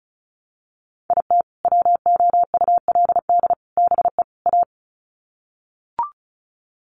Morse code, heard as a keyed tone, about 34 wpm, 700 Hz. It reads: IT WOULD BE A
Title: Sets of 4 Words 35wpm